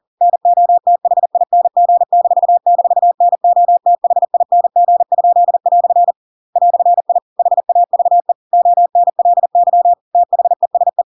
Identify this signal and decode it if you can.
Morse code, heard as a keyed tone, about 40 wpm, 700 Hz. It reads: NOTHING--NOTHING?" "I HAVE ONLY THESE
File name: Pinocchio / Pinocchio ch7 40 40 ik1hge